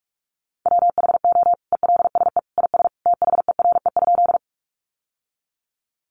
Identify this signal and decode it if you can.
Morse code, heard as a keyed tone, about 44 wpm, 700 Hz. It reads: WHO ELSE IS THERE?
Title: Sentences from Top 800 Words 45wpm